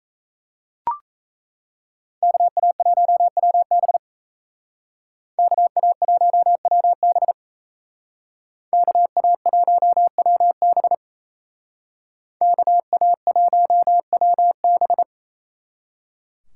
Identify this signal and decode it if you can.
Morse code, heard as a keyed tone, about 35 wpm, 700 Hz. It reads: KA1WB KA1WB KA1WB KA1WB